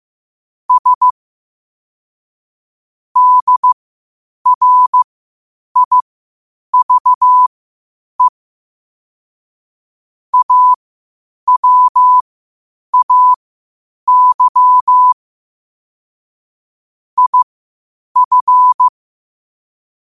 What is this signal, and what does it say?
Morse code, heard as a keyed tone, about 15 wpm, spaced as for 8 wpm, 1000 Hz. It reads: S DRIVE AWAY IF
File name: qotdfiles / 8wpm